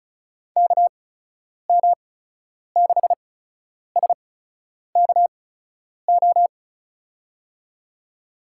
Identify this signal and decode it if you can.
Morse code, heard as a keyed tone, about 36 wpm, spaced as for 10 wpm, 700 Hz. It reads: KM6SKO
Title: USA Call Signs - 8x Character Spacing - 35wpm